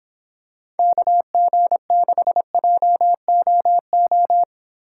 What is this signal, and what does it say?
Morse code, heard as a keyed tone, about 25 wpm, 700 Hz. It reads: KG6JOO